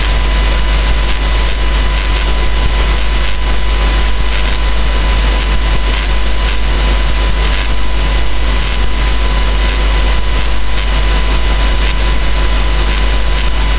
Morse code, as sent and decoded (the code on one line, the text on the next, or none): none